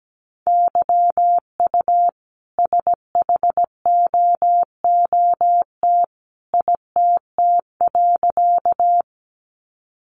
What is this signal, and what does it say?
Morse code, heard as a keyed tone, about 17 wpm, 700 Hz. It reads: YU SHOOT ITT.
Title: A Sound Of Thunder 17wpm